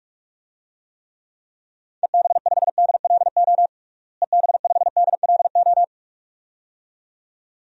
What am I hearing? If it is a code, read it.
Morse code, heard as a keyed tone, about 46 wpm, 700 Hz. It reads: EB5DLO EB5DLO
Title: DXCC-100 Call Signs - 6 Characters - Repeated 2x - 45wpm